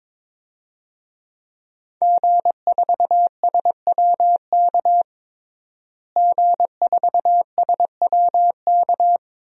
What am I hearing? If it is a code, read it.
Morse code, heard as a keyed tone, about 22 wpm, 700 Hz. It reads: G4SWK G4SWK
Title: DXCC-100 Call Signs - 5 Characters - Repeated 3x - 22wpm